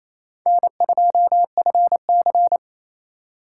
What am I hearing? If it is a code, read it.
Morse code, heard as a keyed tone, about 34 wpm, 700 Hz. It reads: N2FC
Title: Speed Racing - USA Call Signs 28wpm